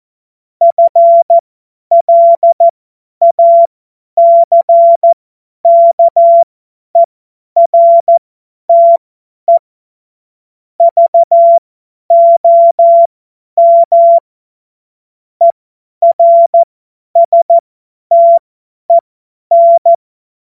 Morse code, heard as a keyed tone, about 14 wpm, spaced as for 10 wpm, 680 Hz. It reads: FLACKERTE VOM ERSTEN
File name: Morsekurs / Witze / Witz 21. März 2023 50 F70